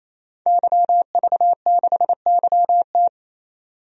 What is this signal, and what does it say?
Morse code, heard as a keyed tone, about 28 wpm, 700 Hz. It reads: YV6YT
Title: DXCC-100 Call Signs - 5 Characters - 28wpm